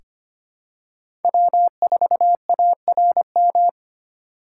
Morse code, heard as a keyed tone, about 25 wpm, 700 Hz. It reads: W4ARM
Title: USA Call Signs - 5 Characters - 3x Character Spacing - 25wpm